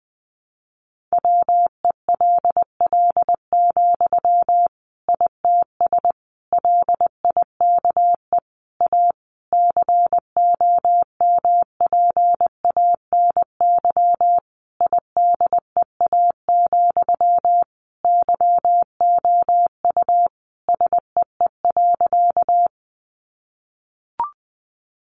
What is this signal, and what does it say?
Morse code, heard as a keyed tone, about 20 wpm, 700 Hz. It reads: WELL, ITS LIKE A COMPANY IDEA, YOU SEE.
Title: Sentences from Top 600 Words 20wpm